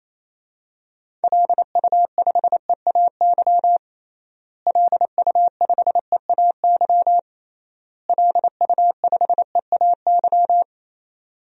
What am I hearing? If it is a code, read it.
Morse code, heard as a keyed tone, about 28 wpm, 700 Hz. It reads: LU5EAY LU5EAY LU5EAY
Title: DXCC-100 Call Signs - 6 Characters - Repeated 3x - 28wpm